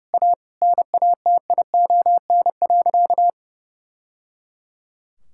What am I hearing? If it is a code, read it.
Morse code, heard as a keyed tone, about 30 wpm, 700 Hz. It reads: A NATION.
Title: Sentences from Top 1000 Words - Review Part 6 - 30wpm